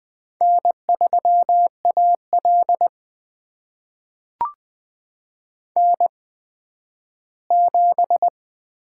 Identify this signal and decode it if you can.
Morse code, heard as a keyed tone, about 20 wpm, spaced as for 12 wpm, 700 Hz. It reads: N3AL N7
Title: USA Call Signs - 4 Characters - 8x Character Spacing - 20wpm